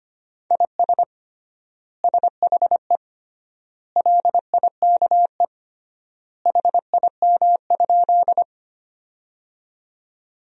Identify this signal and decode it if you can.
Morse code, heard as a keyed tone, about 25 wpm, 700 Hz. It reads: IS SHE LIKE HIM?
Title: 3x Word-Spacing - Sentences from Top 100 Words - 25wpm